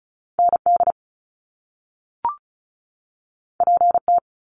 Morse code, heard as a keyed tone, about 36 wpm, 700 Hz. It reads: ND PT
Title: Top Two Letter Combinations in Words 35wpm